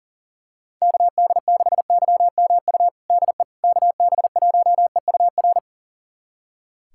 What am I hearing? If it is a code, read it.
Morse code, heard as a keyed tone, about 40 wpm, 700 Hz. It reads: KD6YMU DE KB1EUR
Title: USA Callsign DE Callsign - 6 Characters - 40wpm